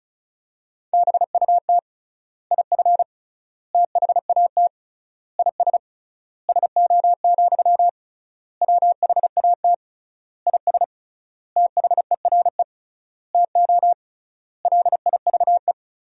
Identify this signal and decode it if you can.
Morse code, heard as a keyed tone, about 34 wpm, 700 Hz. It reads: BUT IF THAT IS SO, WHAT IS THERE TO LIVE
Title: 3x Word-Spacing - Sentences from Top 200 Words - 35wpm